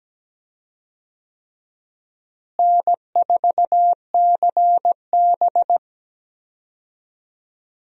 Morse code, heard as a keyed tone, about 17 wpm, 700 Hz. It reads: N4CB
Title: USA Call Signs - 4 Characters - Repeated 3x - 17wpm